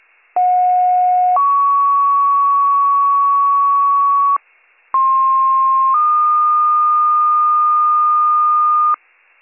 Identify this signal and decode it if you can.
Two-tone page, heard as a one-second tone, then a three-second tone, about 730 then 1120 Hz, about 1060 then 1220 Hz. It is heard from 0.4 to 8.9 s.